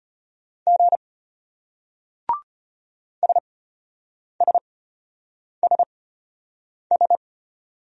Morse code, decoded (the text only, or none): G SSSS